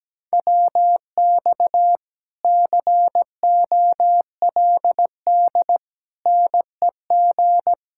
Morse code, heard as a keyed tone, about 17 wpm, 700 Hz. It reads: WX COLD NEG
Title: Speed-Racing Common QSO Phrases Encore - 17wpm